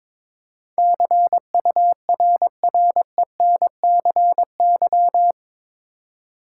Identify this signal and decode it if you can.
Morse code, heard as a keyed tone, about 22 wpm, 700 Hz. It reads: CURRENCY
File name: practice / 22wpm / Eight Letter Words - 22wpm